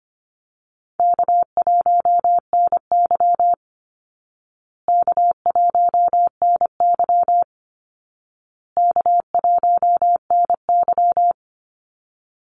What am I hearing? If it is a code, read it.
Morse code, heard as a keyed tone, about 25 wpm, 700 Hz. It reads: K1NY K1NY K1NY
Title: Easy Call Signs Repeated 3x - 25wpm